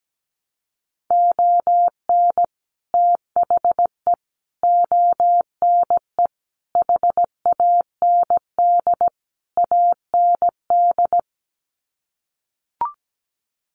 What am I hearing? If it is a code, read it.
Morse code, heard as a keyed tone, about 17 wpm, 700 Hz. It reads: ON THE ONE HAND AND
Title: Sets of 5 Words Encore2 17wpm